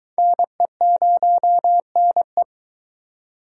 Morse code, heard as a keyed tone, about 21 wpm, 700 Hz. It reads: NE0NE